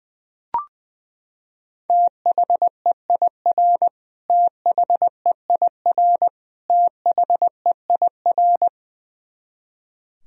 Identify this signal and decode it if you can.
Morse code, heard as a keyed tone, about 20 wpm, 700 Hz. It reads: THEIR THEIR THEIR
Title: Top 100 Words Repeated 20wpm